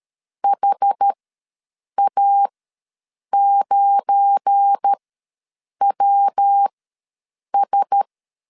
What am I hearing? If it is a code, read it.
Morse code, heard as a keyed tone, about 13 wpm, spaced as for 7 wpm, 800 Hz. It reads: HA9WS
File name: LEZIONI / 7 GF9Z / 7 GF9 ws 12wpm 3344